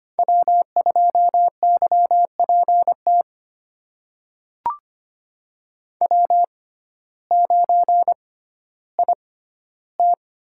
Morse code, heard as a keyed tone, about 25 wpm, spaced as for 14 wpm, 700 Hz. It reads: W2YPT W9IT
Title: USA Call Signs - 6x Character Spacing - 25wpm